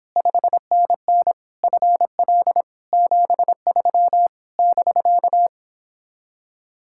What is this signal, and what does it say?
Morse code, heard as a keyed tone, about 31 wpm, 700 Hz. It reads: 5NN FL 73 <BK>